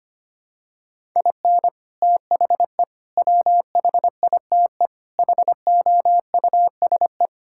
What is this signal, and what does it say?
Morse code, heard as a keyed tone, about 25 wpm, 700 Hz. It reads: IN THE WHITE HOUSE
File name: practice / 25wpm / Sentences with 4-Word Phrases - 25wpm